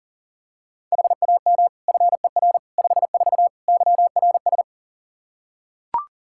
Morse code, heard as a keyed tone, about 40 wpm, 700 Hz. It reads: HAM FER 54 YRS
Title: Practice QSO - Exchange 3 - More Info - 40wpm